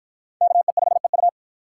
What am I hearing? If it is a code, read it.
Morse code, heard as a keyed tone, about 53 wpm, 700 Hz. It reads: KE5EU